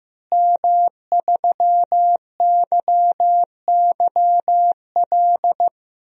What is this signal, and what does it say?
Morse code, heard as a keyed tone, about 15 wpm, 700 Hz. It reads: M3YYL
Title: DXCC-100 Call Signs - 5 Characters - Repeated 2x - 15wpm